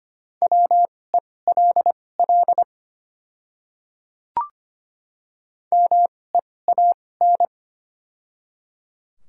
Morse code, heard as a keyed tone, about 25 wpm, spaced as for 18 wpm, 700 Hz. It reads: WELL MEAN
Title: 2x Character Spacing - Top 500 Words - 25wpm